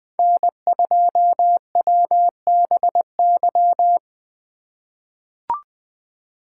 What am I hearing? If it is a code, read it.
Morse code, heard as a keyed tone, about 20 wpm, 700 Hz. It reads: N2WBY
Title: USA Call Signs - 5 Characters - 8x Character Spacing - 20wpm